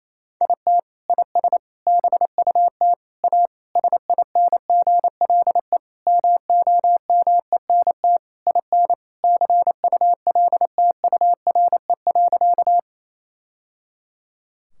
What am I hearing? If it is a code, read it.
Morse code, heard as a keyed tone, about 27 wpm, 700 Hz. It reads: IT IS BUT A SINGLE MOMENT IN CULTURE.
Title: Sentences from Top 1000 Words - Review Part 6 - 28wpm